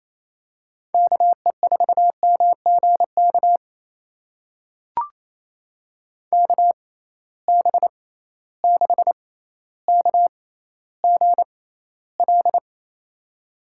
Morse code, heard as a keyed tone, about 28 wpm, spaced as for 14 wpm, 700 Hz. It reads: KE4MGK KB6KGL
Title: USA Call Signs - 6 Characters - 6x Character Spacing - 28wpm